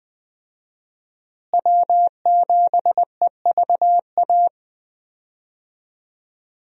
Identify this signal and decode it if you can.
Morse code, heard as a keyed tone, about 21 wpm, 700 Hz. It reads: W7EVA
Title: USA Call Signs - 5 Characters - Repeated 3x - 20wpm